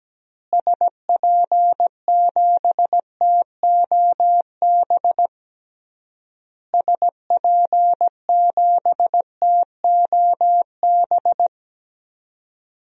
Morse code, heard as a keyed tone, about 17 wpm, 700 Hz. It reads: SP7TOB SP7TOB
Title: DXCC-100 Call Signs - 6 Characters - Repeated 3x - 17wpm